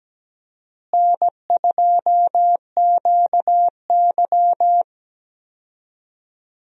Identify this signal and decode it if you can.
Morse code, heard as a keyed tone, about 17 wpm, 700 Hz. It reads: N2QY
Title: USA Call Signs - 4 Characters - Repeated 2x - 17wpm